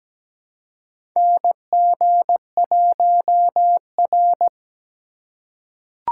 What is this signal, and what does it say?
Morse code, heard as a keyed tone, about 17 wpm, 700 Hz. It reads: NG1R
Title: USA Call Signs - 4 Characters - 3x Character Spacing - 17wpm